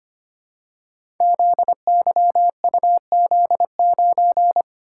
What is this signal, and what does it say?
Morse code, heard as a keyed tone, about 25 wpm, 700 Hz. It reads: ZYUZ9
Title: Code Groups - 5 Letter:Numbers - 25wpm